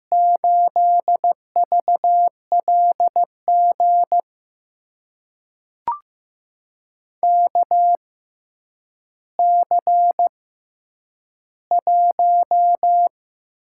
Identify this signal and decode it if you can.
Morse code, heard as a keyed tone, about 15 wpm, spaced as for 9 wpm, 700 Hz. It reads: 8VLG KC1
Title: USA Call Signs - 5 Characters - 6x Character Spacing - 15wpm